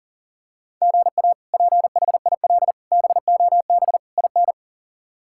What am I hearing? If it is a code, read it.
Morse code, heard as a keyed tone, about 40 wpm, 700 Hz. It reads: GA PHIL BOB IN
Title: K1USN SST QSO Practice - 40wpm